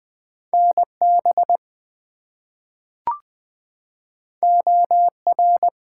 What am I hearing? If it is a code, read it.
Morse code, heard as a keyed tone, about 20 wpm, 700 Hz. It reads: NB OR
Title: US States and Canadian Provinces 20wpm